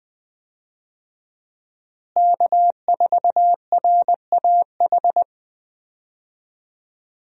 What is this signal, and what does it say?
Morse code, heard as a keyed tone, about 20 wpm, 700 Hz. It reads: K4RAH
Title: USA Call Signs - 5 Characters - Repeated 3x - 20wpm